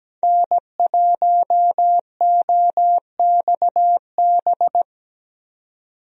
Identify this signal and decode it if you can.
Morse code, heard as a keyed tone, about 17 wpm, 700 Hz. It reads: N1OXB
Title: USA Call Signs - 5 Characters - 17wpm